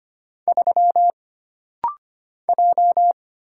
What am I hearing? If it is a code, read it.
Morse code, heard as a keyed tone, about 25 wpm, 700 Hz. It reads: 3 J
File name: Single Character (Letter:Number:Punctuation:Prosign) - Rapid-Fire - 25wpm